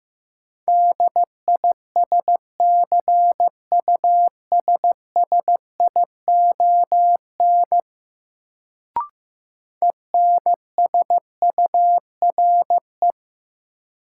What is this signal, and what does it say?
Morse code, heard as a keyed tone, about 15 wpm, 700 Hz. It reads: DISCUSSION ENSURE
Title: Sentences from Top 1100 Words - New Word Drill - 15wpm